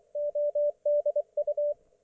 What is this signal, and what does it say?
Morse code, heard as a keyed tone, about 24 wpm, 590 Hz, with radio noise. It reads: ODU
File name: Archiv / CUC212 / ok1fhi cul212